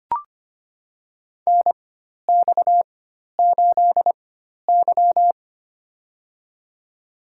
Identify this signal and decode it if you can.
Morse code, heard as a keyed tone, about 25 wpm, spaced as for 12 wpm, 700 Hz. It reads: NX8Y
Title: USA Call Signs - 4 Characters - 4x Character Spacing - 25wpm